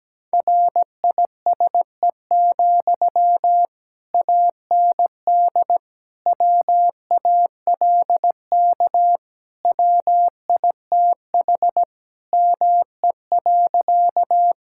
Morse code, heard as a keyed tone, about 17 wpm, 700 Hz. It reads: RISE, AND WALK WITH ME.
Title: Sentences from Top 1000 Words - Review Part 6 - 17wpm